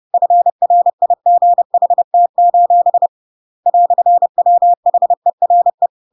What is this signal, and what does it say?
Morse code, heard as a keyed tone, about 30 wpm, 700 Hz. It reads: FRIGHT: "WHERE
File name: Pinocchio / Pinocchio ch1 30 30 ik1hge